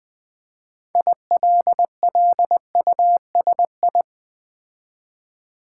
Morse code, heard as a keyed tone, about 20 wpm, 700 Hz. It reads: ILLUSI
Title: Build A Word - 3 Letters At A Time - 8-Letter Words - 20wpm